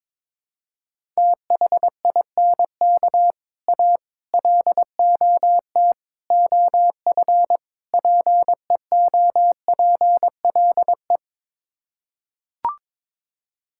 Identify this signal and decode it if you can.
Morse code, heard as a keyed tone, about 22 wpm, 700 Hz. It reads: THINK A LOT OF PEOPLE
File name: practice / 22wpm / Build A Phrase - 5-Word Phrases - 22wpm